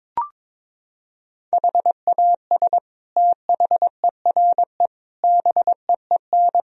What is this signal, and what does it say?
Morse code, heard as a keyed tone, about 22 wpm, 700 Hz. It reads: HAS THERE BEEN
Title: Sentences from Top 1000 Words - Review Part 4 - 22wpm